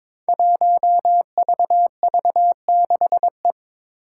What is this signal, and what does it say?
Morse code, heard as a keyed tone, about 22 wpm, 700 Hz. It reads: 1VV6E
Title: Five Challenging Letter-Number Combinations 22wpm